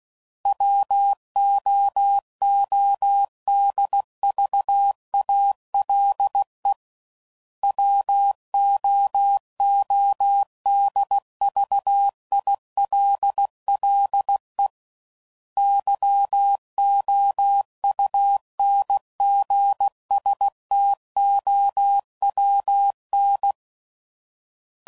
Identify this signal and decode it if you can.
Morse code, heard as a keyed tone, about 16 wpm, 800 Hz. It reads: WOODVALE WOODVILLE YOUNGSTOWN
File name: CW Code Groups @15WPM / fromSM's / Cities